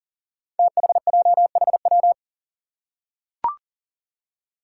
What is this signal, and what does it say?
Morse code, heard as a keyed tone, about 37 wpm, 700 Hz. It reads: THJHW
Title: Five Challenging Letter-Number Combinations 40wpm